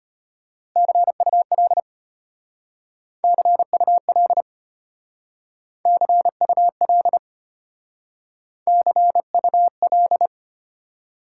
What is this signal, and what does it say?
Morse code, heard as a keyed tone, about 32 wpm, 700 Hz. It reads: CUL CUL CUL CUL